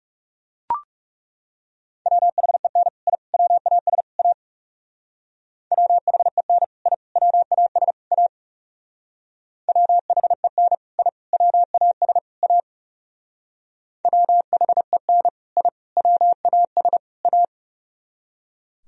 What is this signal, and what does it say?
Morse code, heard as a keyed tone, about 36 wpm, 700 Hz. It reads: WHEN I WAS A WHEN I WAS A WHEN I WAS A WHEN I WAS A
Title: Speed-Racing - Sets of 4 Words 30wpm